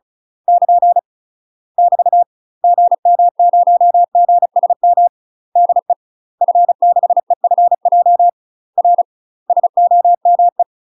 Morse code, heard as a keyed tone, about 36 wpm, 700 Hz. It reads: <KN> <BT> GM0GSM DE F6EFJ R SOME
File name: UK MASTER QSOs TRAINING / 35WPM / UK MASTER QSOs 0035